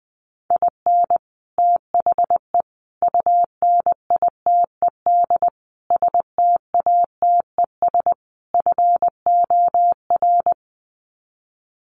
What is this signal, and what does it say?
Morse code, heard as a keyed tone, about 19 wpm, 700 Hz. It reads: IN THE UNITED STATES FOR
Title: Sets of 5 Words Encore2 20wpm